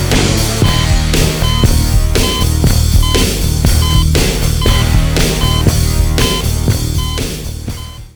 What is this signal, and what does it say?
There is no code.